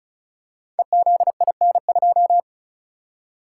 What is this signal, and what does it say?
Morse code, heard as a keyed tone, about 36 wpm, 700 Hz. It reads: EZIN2
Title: Code Groups - 5 Letter:Numbers - 35wpm